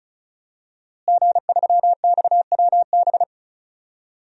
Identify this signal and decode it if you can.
Morse code, heard as a keyed tone, about 36 wpm, 700 Hz. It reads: G3XWB
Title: DXCC-100 Call Signs - 5 Characters - Repeated 3x - 35wpm